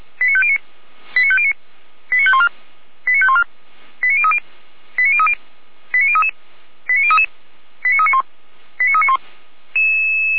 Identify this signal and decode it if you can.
Five-tone selective call (ZVEI), heard as quick runs of five tones, tones of about 80 ms. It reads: 89509 89509 89514 89514 89030 89030 89033 89033 89391 89391
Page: /455/SDR/ZVEI/